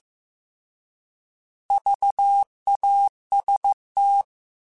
Morse code, heard as a keyed tone, about 15 wpm, 800 Hz. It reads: VAST
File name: Alphabetical / Alphabetical mp3s / Vs @13